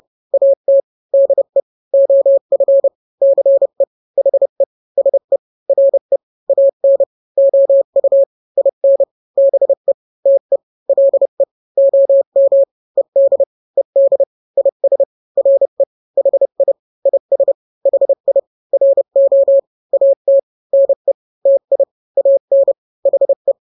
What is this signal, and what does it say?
Morse code, heard as a keyed tone, about 30 wpm, 550 Hz. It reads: AT DE OF CE HE SE RE AN OU IN BE TE LE OM ED ED IS RE HI IS HI RO AT NE TI AN HE
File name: n-grams / 2-grams 30 25 WPM